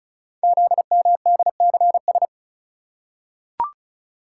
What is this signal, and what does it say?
Morse code, heard as a keyed tone, about 36 wpm, 690 Hz. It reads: ZMDCS
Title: Code Groups - 5 Letters - 35wpm